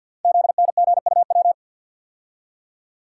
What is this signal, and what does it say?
Morse code, heard as a keyed tone, about 49 wpm, 700 Hz. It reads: ZNGUW
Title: Code Groups - 5 Letters - 50wpm